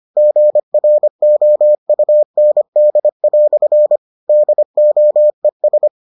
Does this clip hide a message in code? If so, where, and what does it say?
Yes: Morse code from 0.2 to 5.9 s, heard as a keyed tone, about 25 wpm, 600 Hz. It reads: GROUND" DOES